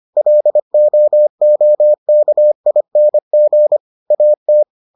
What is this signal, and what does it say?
Morse code, heard as a keyed tone, about 25 wpm, 600 Hz. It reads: LOOKING AT